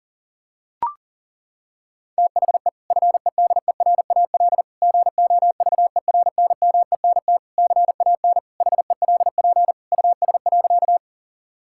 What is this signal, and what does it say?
Morse code, heard as a keyed tone, about 40 wpm, 700 Hz. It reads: THE FEDERAL GOVERNMENT CAN HELP US.
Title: Sentences from Top 1000 Words - Review Part 5 - 40wpm